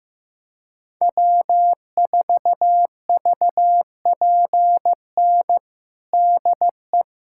Morse code, heard as a keyed tone, about 16 wpm, 700 Hz. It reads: W4VPN DE
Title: USA Callsign DE Callsign - 5 Characters - 15wpm